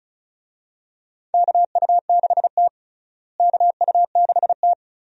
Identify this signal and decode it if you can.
Morse code, heard as a keyed tone, about 34 wpm, 700 Hz. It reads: KU6T KU6T
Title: USA Call Signs - 4 Characters - Repeated 2x - 35wpm